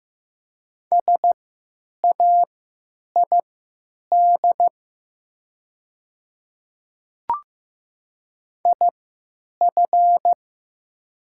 Morse code, heard as a keyed tone, about 15 wpm, spaced as for 9 wpm, 690 Hz. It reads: SAID IF
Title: 3x Character Spacing - Top 100 Words - 15wpm